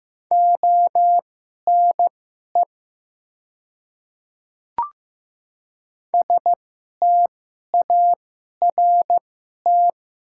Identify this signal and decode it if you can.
Morse code, heard as a keyed tone, about 15 wpm, spaced as for 11 wpm, 700 Hz. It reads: ONE START
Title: 2x Character Spacing - Top 300 Words - 15wpm